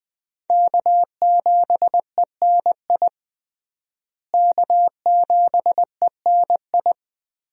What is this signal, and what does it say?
Morse code, heard as a keyed tone, about 21 wpm, 700 Hz. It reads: K7ENI K7ENI
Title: USA Call Signs - 5 Characters - Repeated 2x - 20wpm